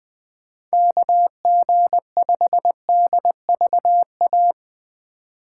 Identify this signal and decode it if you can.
Morse code, heard as a keyed tone, about 22 wpm, 700 Hz. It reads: KG5DVA
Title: Speed Racing - USA 2x3 Call Signs 20wpm